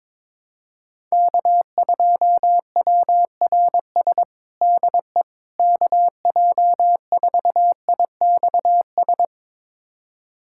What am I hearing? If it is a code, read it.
Morse code, heard as a keyed tone, about 22 wpm, 700 Hz. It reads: K2WRS DE KJ4IXS
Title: USA Callsign DE Callsign 22wpm